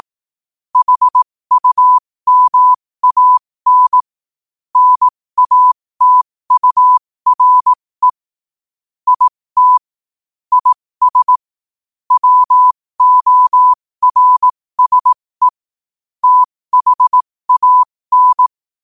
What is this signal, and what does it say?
Morse code, heard as a keyed tone, about 18 wpm, spaced as for 15 wpm, 1000 Hz. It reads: HUMAN NATURE IT IS WORSE THAN
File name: qotdfiles / 15wpm